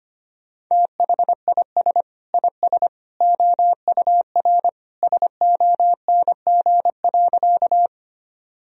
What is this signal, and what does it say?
Morse code, heard as a keyed tone, about 25 wpm, 700 Hz. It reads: THIS IS OUR SONG.
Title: Sentences from Top 1000 Words - Review Part 2 - 25wpm